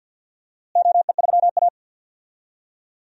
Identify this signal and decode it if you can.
Morse code, heard as a keyed tone, about 51 wpm, 700 Hz. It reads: OE3A
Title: DXCC-100 Call Signs - 4 Characters - Repeated 2x - 50wpm